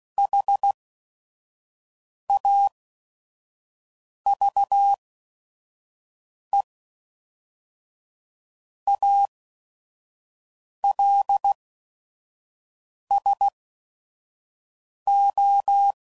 Morse code, heard as a keyed tone, about 16 wpm, spaced as for 5 wpm, 800 Hz. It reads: HAVE ALSO
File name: CW Code Groups @5WPM / Weather Sets / Weather 5 5